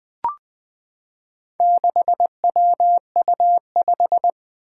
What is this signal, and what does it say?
Morse code, heard as a keyed tone, about 21 wpm, 700 Hz. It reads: 6WU5
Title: Four Challenging Letter-Number Combinations 20wpm